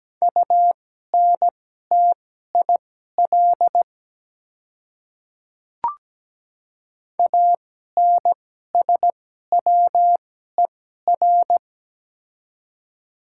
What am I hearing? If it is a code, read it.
Morse code, heard as a keyed tone, about 17 wpm, spaced as for 12 wpm, 700 Hz. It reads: UNTIL ANSWER
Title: 2x Character Spacing - Top 300 Words - 17wpm